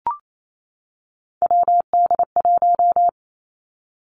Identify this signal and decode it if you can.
Morse code, heard as a keyed tone, about 28 wpm, 700 Hz. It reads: WD1
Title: Three Challenging Letter-Number Combinations 28wpm